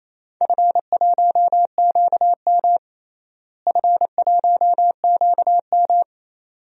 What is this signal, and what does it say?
Morse code, heard as a keyed tone, about 28 wpm, 700 Hz. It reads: F1QM F1QM
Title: DXCC-100 Call Signs - 4 Characters - Repeated 2x - 28wpm